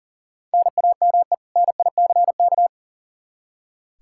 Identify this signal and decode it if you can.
Morse code, heard as a keyed tone, about 39 wpm, 700 Hz. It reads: NAME NICK
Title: Practice QSO - Exchange 1 - Beginning - 40wpm